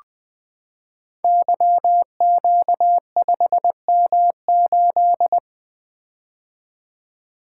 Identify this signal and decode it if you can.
Morse code, heard as a keyed tone, about 20 wpm, 700 Hz. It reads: YQ5M8
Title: Code Groups - 5 Letter:Numbers - 20wpm